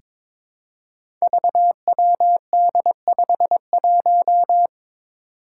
Morse code, heard as a keyed tone, about 23 wpm, 700 Hz. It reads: VWD51
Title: Five Challenging Letter-Number Combinations 22wpm